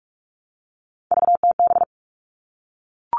Morse code, heard as a keyed tone, about 44 wpm, 700 Hz. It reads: VTB
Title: All Cut Numbers - 3-Digit 45wpm